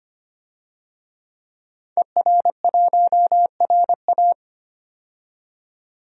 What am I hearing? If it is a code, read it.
Morse code, heard as a keyed tone, about 25 wpm, 690 Hz. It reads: ER1RA
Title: DXCC-100 Call Signs - 5 Characters - 25wpm